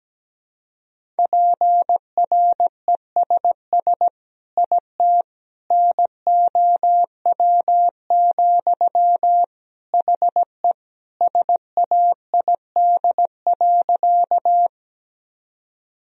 Morse code, heard as a keyed tone, about 17 wpm, 700 Hz. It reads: PRESS IT NOW, HE SAID.
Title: Sentences from Top 1000 Words - Review Part 1 - 17wpm